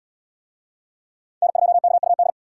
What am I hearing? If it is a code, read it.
Morse code, heard as a keyed tone, about 65 wpm, 700 Hz. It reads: N2KUD